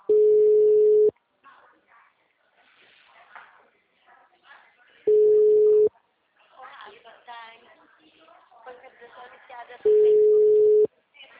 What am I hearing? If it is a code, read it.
Ringback tone (Europe, 425 Hz), with speech.